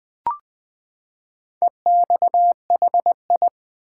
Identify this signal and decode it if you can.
Morse code, heard as a keyed tone, about 20 wpm, 700 Hz. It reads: EXHI
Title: Build A Word - 4 Letters At A Time - 10-Letter Words - 20wpm